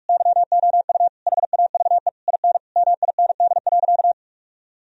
Morse code, heard as a keyed tone, about 46 wpm, 700 Hz. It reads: YOU HAVE IN MIND.
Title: Sentences from Top 1000 Words - Review Part 3 - 45wpm